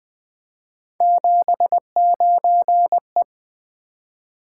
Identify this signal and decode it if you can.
Morse code, heard as a keyed tone, about 20 wpm, 700 Hz. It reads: 79E
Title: Three Letter-Number Combinations - 20wpm